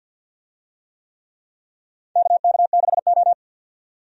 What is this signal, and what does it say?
Morse code, heard as a keyed tone, about 47 wpm, 700 Hz. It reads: KK6O
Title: USA Call Signs - 4 Characters - 50wpm